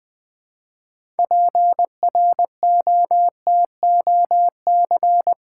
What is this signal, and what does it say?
Morse code, heard as a keyed tone, about 20 wpm, 700 Hz. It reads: PROTOC
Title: Build A Word - 2 Letters At A Time - 8-Letter Words - 20wpm